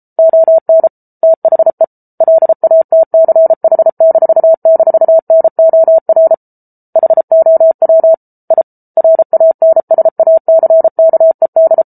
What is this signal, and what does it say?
Morse code, heard as a keyed tone, about 34 wpm, 650 Hz. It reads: ON THE LATCH--NOR HOW I RANSACKED